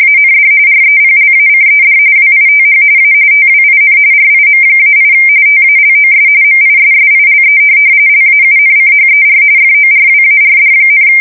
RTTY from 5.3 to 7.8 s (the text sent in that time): TALOS MISSILE